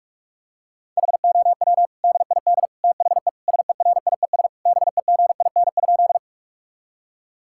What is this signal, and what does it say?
Morse code, heard as a keyed tone, about 41 wpm, 700 Hz. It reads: HOW DID THE SERIES BEGIN?
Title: Sentences from Top 1000 Words - Review Part 5 - 45wpm